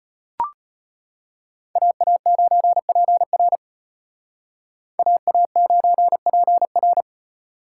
Morse code, heard as a keyed tone, about 30 wpm, 700 Hz. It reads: AA9PR AA9PR
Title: Speed Racing - USA Call Signs 25wpm